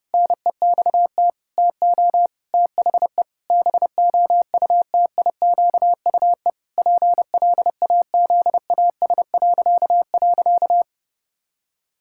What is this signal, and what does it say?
Morse code, heard as a keyed tone, about 30 wpm, 700 Hz. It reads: NEXT TO THE BOUTIQUE PLAZAS..
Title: Pangram Challenge v2 - 30wpm